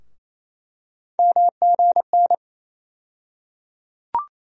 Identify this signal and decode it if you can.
Morse code, heard as a keyed tone, about 28 wpm, 700 Hz. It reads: MGN
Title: USA Call Sign Postfixes - 3 Letters 28wpm